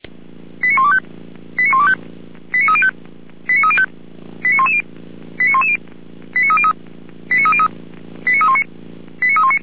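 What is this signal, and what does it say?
Five-tone selective call (ZVEI), heard as quick runs of five tones, tones of about 70 ms. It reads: 89125 89125 89305 89305 89110 89110 89303 89303 89319 89319
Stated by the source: ZVEI five-tone call received on an SDR.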